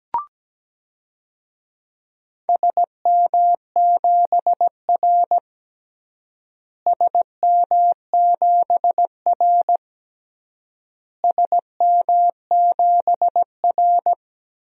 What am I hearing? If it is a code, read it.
Morse code, heard as a keyed tone, about 17 wpm, 700 Hz. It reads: SM7R SM7R SM7R
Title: DXCC-100 Call Signs - 4 Characters - Repeated 3x - 17wpm